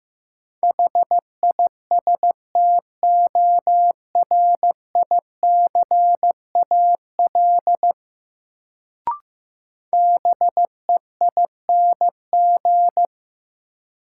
Morse code, heard as a keyed tone, about 15 wpm, 700 Hz. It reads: HISTORICAL BEING
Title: Sentences from Top 600 Words - New Word Drill - 15wpm